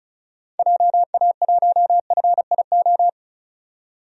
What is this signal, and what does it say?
Morse code, heard as a keyed tone, about 34 wpm, 700 Hz. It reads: JA1FIO
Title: DXCC-100 Call Signs - 6 Characters - 35wpm